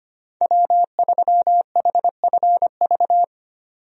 Morse code, heard as a keyed tone, about 25 wpm, 690 Hz. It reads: W3HFV
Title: USA Call Signs - 5 Characters - 25wpm